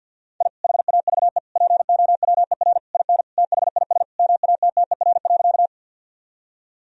Morse code, heard as a keyed tone, about 50 wpm, 700 Hz. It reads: I HAVE POWER IN THIS MATTER.
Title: Sentences from Top 1000 Words - Review Part 2 - 50wpm